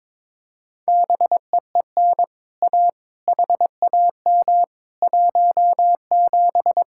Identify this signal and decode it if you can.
Morse code, heard as a keyed tone, about 22 wpm, 700 Hz. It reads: BEEN A HAM 17
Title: Speed-Racing Common QSO Phrases - 22wpm